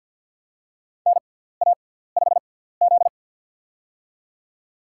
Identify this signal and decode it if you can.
Morse code, heard as a keyed tone, about 47 wpm, spaced as for 17 wpm, 700 Hz. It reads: NA5Z
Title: USA Call Signs - 4 Characters - 6x Character Spacing - 50wpm